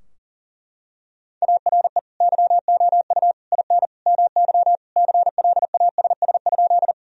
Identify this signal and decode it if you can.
Morse code, heard as a keyed tone, about 40 wpm, 700 Hz. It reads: ARE YOU IN MY CLASS?
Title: Sentences from Top 1000 Words - Review Part 2 - 40wpm